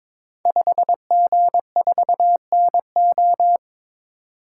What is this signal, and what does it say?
Morse code, heard as a keyed tone, about 22 wpm, 700 Hz. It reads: 5G4NO
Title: Code Groups - 5 Letter:Numbers - 22wpm